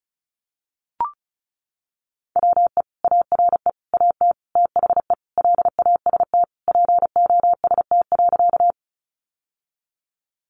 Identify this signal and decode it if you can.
Morse code, heard as a keyed tone, about 34 wpm, 700 Hz. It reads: WE ARE AT THE LAST POST.
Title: Sentences from Top 1000 Words 35wpm